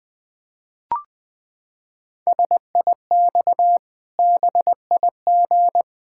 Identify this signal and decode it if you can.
Morse code, heard as a keyed tone, about 21 wpm, 700 Hz. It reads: SIX BIG
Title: Pangram Challenge v2 - 20wpm